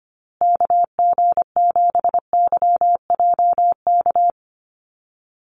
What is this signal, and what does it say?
Morse code, heard as a keyed tone, about 25 wpm, 700 Hz. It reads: KG7YJK
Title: USA Call Signs Repeated 3x 25wpm